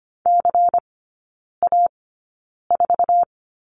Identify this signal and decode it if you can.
Morse code, heard as a keyed tone, about 25 wpm, spaced as for 9 wpm, 700 Hz. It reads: CA4
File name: course / 25:10wpm / 093 Lesson 16e - Letter c - Three Characters 25wpm